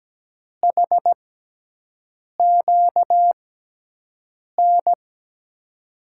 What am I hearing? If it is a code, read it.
Morse code, heard as a keyed tone, about 17 wpm, spaced as for 6 wpm, 700 Hz. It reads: HQN
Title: USA Call Signs - 5 Characters - 6x Character Spacing - 17wpm